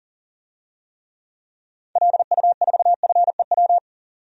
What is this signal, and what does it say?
Morse code, heard as a keyed tone, about 39 wpm, 700 Hz. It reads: LU4FEW
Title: DXCC-100 Call Signs - 6 Characters - 40wpm